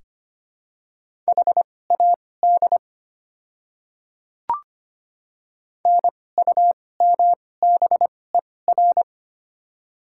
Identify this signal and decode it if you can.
Morse code, heard as a keyed tone, about 25 wpm, spaced as for 18 wpm, 700 Hz. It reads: HAD NUMBER
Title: 2x Character Spacing - Top 100 Words - 25wpm